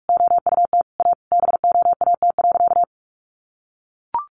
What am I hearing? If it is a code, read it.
Morse code, heard as a keyed tone, about 44 wpm, 700 Hz. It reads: OUT A BOAT.
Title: Sentences from Top 1500 Words 45wpm